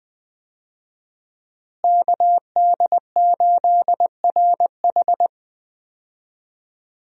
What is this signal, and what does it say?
Morse code, heard as a keyed tone, about 20 wpm, 700 Hz. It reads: KD8RH
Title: USA Call Signs - 5 Characters - Repeated 2x - 20wpm